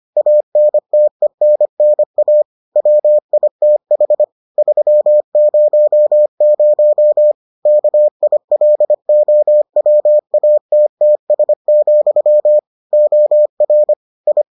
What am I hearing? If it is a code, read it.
Morse code, heard as a keyed tone, about 25 wpm, 600 Hz. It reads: ANTENNA WITH 300 KILOWATTS, OR I